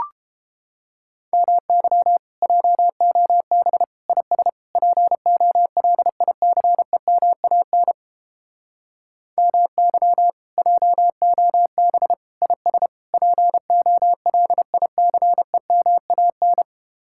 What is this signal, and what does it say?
Morse code, heard as a keyed tone, about 29 wpm, 700 Hz. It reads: MY JOB IS POLICEMAN MY JOB IS POLICEMAN